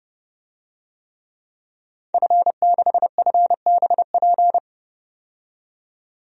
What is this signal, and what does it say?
Morse code, heard as a keyed tone, about 30 wpm, 700 Hz. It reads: F6FBP
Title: DXCC-100 Call Signs - 5 Characters - 30wpm